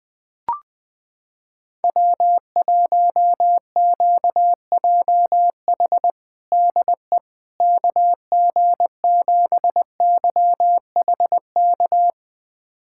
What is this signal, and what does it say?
Morse code, heard as a keyed tone, about 20 wpm, 700 Hz. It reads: W1QJH DE KG7YHK
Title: USA Callsign DE Callsign 20wpm